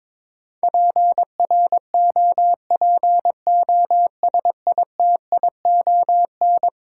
Morse code, heard as a keyed tone, about 22 wpm, 700 Hz. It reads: PROPOSITION
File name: practice / 22wpm / Eleven Letter Words - 22wpm